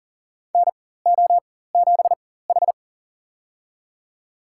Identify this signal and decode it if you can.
Morse code, heard as a keyed tone, about 40 wpm, spaced as for 19 wpm, 700 Hz. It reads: NO7H
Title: USA Call Signs - 4 Characters - 4x Character Spacing - 40wpm